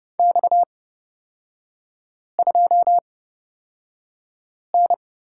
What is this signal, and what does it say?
Morse code, heard as a keyed tone, about 29 wpm, spaced as for 5 wpm, 690 Hz. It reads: X2N
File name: Lesson 37e - Letter x - Three Characters 30wpm